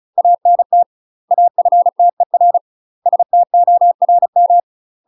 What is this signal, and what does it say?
Morse code, heard as a keyed tone, about 35 wpm, 700 Hz. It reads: ANT AFTER STORM
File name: UK MASTER QSOs TRAINING / 35WPM / UK MASTER QSOs 0043